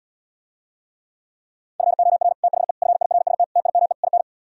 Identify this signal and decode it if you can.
Morse code, heard as a keyed tone, about 77 wpm, 700 Hz. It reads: FOR THE FIRST TIME IN